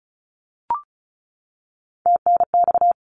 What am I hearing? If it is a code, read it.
Morse code, heard as a keyed tone, about 36 wpm, 700 Hz. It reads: TNX
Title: Abbreviations 35wpm